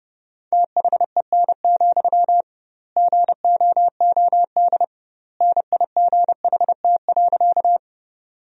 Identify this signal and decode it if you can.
Morse code, heard as a keyed tone, about 32 wpm, 700 Hz. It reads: THEN, GOOD NIGHT.
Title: 2x Word-Spacing - Sentences from Top 300 Words - 30wpm